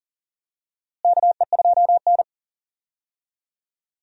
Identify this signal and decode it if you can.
Morse code, heard as a keyed tone, about 40 wpm, 700 Hz. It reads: KE2N
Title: USA Call Signs - 4 Characters - Repeated 3x - 40wpm